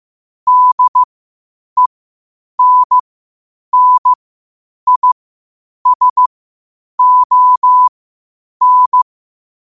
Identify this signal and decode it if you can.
Morse code, heard as a keyed tone, about 15 wpm, spaced as for 8 wpm, 1000 Hz. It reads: DENNISON
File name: qotdfiles / 8wpm